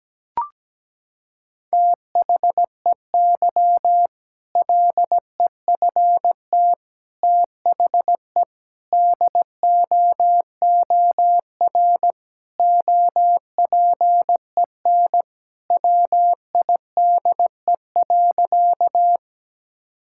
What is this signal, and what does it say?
Morse code, heard as a keyed tone, about 17 wpm, 700 Hz. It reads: THEY LEFT THE DOOR OPEN WIDE.
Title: Sentences from Top 1000 Words - Review Part 6 - 17wpm